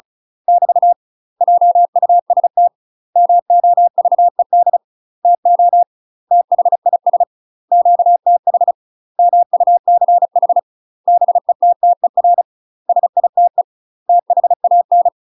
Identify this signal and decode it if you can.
Morse code, heard as a keyed tone, about 36 wpm, 700 Hz. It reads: <BT> JUST MOVED TO THIS QTH MUCH BETTER SITE THAN